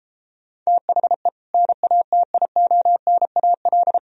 Morse code, heard as a keyed tone, about 33 wpm, 700 Hz. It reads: THE NATIONAL